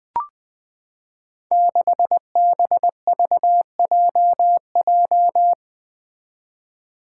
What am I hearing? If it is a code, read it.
Morse code, heard as a keyed tone, about 21 wpm, 700 Hz. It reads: 6BVJJ
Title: Five Challenging Letter-Number Combinations 20wpm